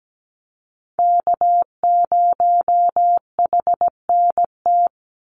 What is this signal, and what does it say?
Morse code, heard as a keyed tone, about 17 wpm, 700 Hz. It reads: K0HNT
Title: USA Call Signs 17wpm